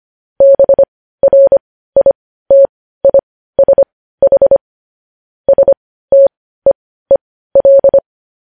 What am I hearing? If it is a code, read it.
Morse code, heard as a keyed tone, about 25 wpm, spaced as for 15 wpm, 550 Hz. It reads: BRITISH STEEL
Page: news-headlines-in-morse-code-at-15-wpm